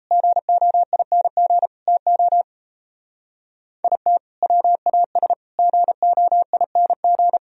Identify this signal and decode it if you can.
Morse code, heard as a keyed tone, about 37 wpm, 700 Hz. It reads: GOING TO IT WAS GOING